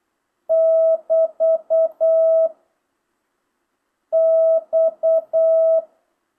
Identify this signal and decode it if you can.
Morse code, heard as a keyed tone, about 8 wpm, 650 Hz. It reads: <BT> X
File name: jh8cbh / 2025cw